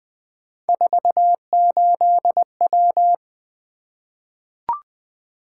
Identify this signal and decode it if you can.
Morse code, heard as a keyed tone, about 20 wpm, 700 Hz. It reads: 48W
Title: Three Letter-Number Combinations - 20wpm